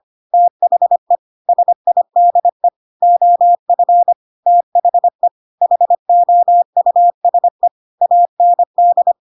Morse code, heard as a keyed tone, about 25 wpm, 700 Hz. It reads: THE SIDE OF THE HOUSE AND
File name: Pinocchio / Pinocchio ch7 25 25 ik1hge